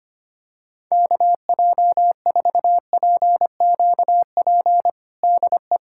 Morse code, heard as a keyed tone, about 25 wpm, 700 Hz. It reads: KJ4PQP DE
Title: USA Callsign DE Callsign - 6 Characters - 25wpm